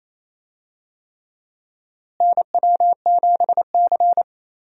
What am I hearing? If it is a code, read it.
Morse code, heard as a keyed tone, about 28 wpm, 700 Hz. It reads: NW7C
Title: USA Call Signs - 4 Characters - Repeated 3x - 28wpm